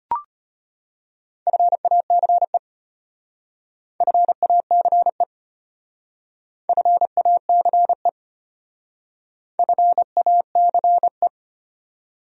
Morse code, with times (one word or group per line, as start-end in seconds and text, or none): 1.5-2.6 FACE
4.0-5.2 FACE
6.7-8.1 FACE
9.6-11.3 FACE